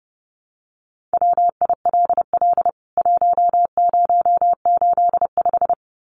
Morse code, heard as a keyed tone, about 28 wpm, 700 Hz. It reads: WILL 1085
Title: CWT Exchange 30wpm